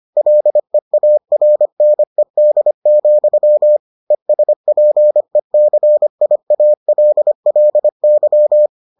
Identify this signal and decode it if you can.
Morse code, heard as a keyed tone, about 25 wpm, 600 Hz. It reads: LEARNED, ESPECIALLY